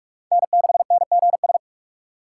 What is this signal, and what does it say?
Morse code, heard as a keyed tone, about 46 wpm, 700 Hz. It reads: N6NGS
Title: USA Call Signs - 5 Characters - 3x Character Spacing - 45wpm